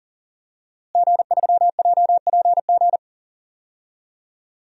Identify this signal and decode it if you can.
Morse code, heard as a keyed tone, about 37 wpm, 700 Hz. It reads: G3JPG
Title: DXCC-100 Call Signs - 5 Characters - Repeated 2x - 40wpm